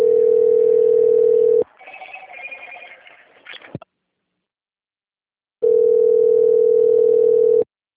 Ringback tone (North America), 440 + 480 Hz.